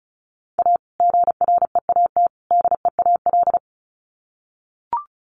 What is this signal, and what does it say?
Morse code, heard as a keyed tone, about 34 wpm, 700 Hz. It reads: A GREAT DEAL
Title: Sets of 3 Words 35wpm